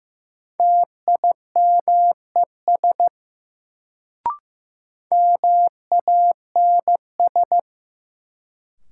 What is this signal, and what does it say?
Morse code, heard as a keyed tone, about 15 wpm, 700 Hz. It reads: TIMES MANS
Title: Sentences from Top 1000 Words - Review Part 6 - New Word Drill - 15wpm